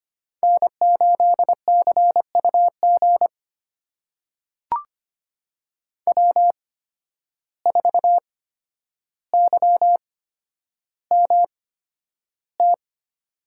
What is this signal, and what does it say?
Morse code, heard as a keyed tone, about 25 wpm, spaced as for 11 wpm, 700 Hz. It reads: N8CUG W4YMT
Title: USA Call Signs - 5 Characters - 8x Character Spacing - 25wpm